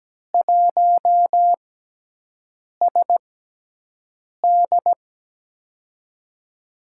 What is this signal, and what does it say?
Morse code, heard as a keyed tone, about 17 wpm, spaced as for 6 wpm, 700 Hz. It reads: 1SD
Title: USA Call Signs - 6x Character Spacing - 17wpm